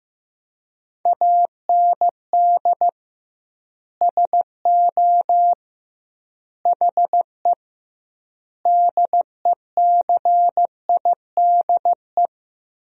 Morse code, heard as a keyed tone, about 15 wpm, 700 Hz. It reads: AND SO HE DECIDE
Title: 2x Word-Spacing - Sentences from Top 500 Words - 15wpm